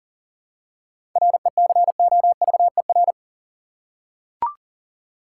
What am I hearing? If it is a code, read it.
Morse code, heard as a keyed tone, about 40 wpm, 700 Hz. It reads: RECOVER
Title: Seven Letter Words - 40wpm